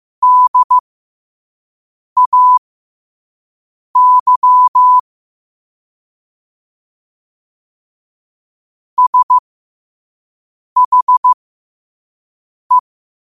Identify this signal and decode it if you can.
Morse code, heard as a keyed tone, about 15 wpm, spaced as for 5 wpm, 1000 Hz. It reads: DAY SHE
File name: qotdfiles / 5wpm